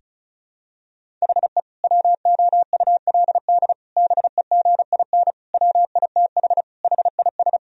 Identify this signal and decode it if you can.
Morse code, heard as a keyed tone, about 34 wpm, 690 Hz. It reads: HE WOULD BEGIN WITH HIS
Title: Sentences from Top 1000 Words - Review Part 1 - 35wpm